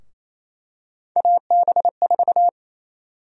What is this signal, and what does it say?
Morse code, heard as a keyed tone, about 28 wpm, 700 Hz. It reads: AB4
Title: USA Call Sign Prefixes - 28wpm